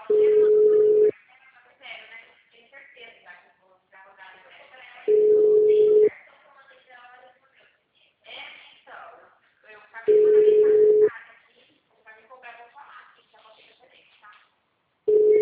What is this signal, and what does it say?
Ringback tone (Europe, 425 Hz), with speech.
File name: out- -2001- - - .322